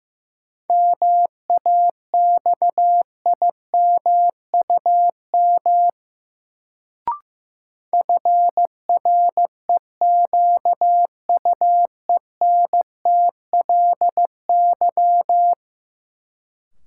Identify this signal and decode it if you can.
Morse code, heard as a keyed tone, about 15 wpm, 700 Hz. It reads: MAXIMUM FREQUENTLY
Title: Sentences from Top 1300 Words - New Word Drill Alt - 15wpm